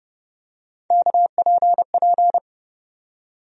Morse code, heard as a keyed tone, about 30 wpm, 700 Hz. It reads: KPP
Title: USA Call Sign Postfixes - 3 Letters 30wpm